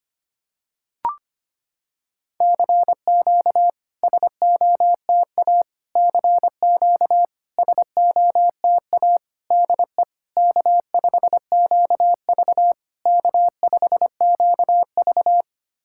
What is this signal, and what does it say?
Morse code, heard as a keyed tone, about 25 wpm, 690 Hz. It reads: CQ SOTA CQ SOTA DE K5QV K5QV
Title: SOTA QSO Practice 25wpm